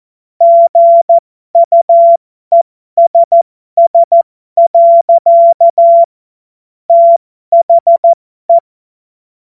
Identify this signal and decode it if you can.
Morse code, heard as a keyed tone, about 14 wpm, spaced as for 12 wpm, 680 Hz. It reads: GUESS. THE